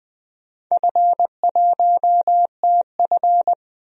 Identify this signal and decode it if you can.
Morse code, heard as a keyed tone, about 20 wpm, 700 Hz. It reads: F1TF
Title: DXCC-100 Call Signs - 4 Characters - 20wpm